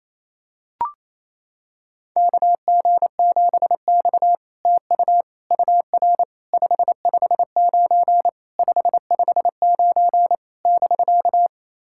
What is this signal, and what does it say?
Morse code, heard as a keyed tone, about 28 wpm, 700 Hz. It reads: KG7X TU UR 559 559 <BK>
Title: SOTA QSO Practice 28wpm